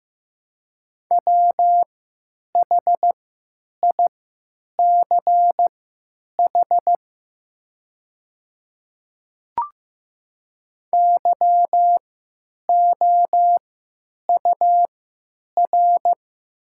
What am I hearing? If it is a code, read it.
Morse code, heard as a keyed tone, about 15 wpm, spaced as for 9 wpm, 700 Hz. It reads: WHICH YOUR
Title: 3x Character Spacing - Top 200 Words - 15wpm